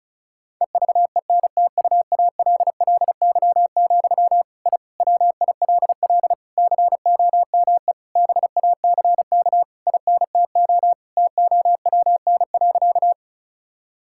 Morse code, heard as a keyed tone, about 36 wpm, 700 Hz. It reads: EVENTUALLY, I WILL COME BACK INTO TOWN.
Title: Sentences from Top 1000 Words - Review Part 6 - 35wpm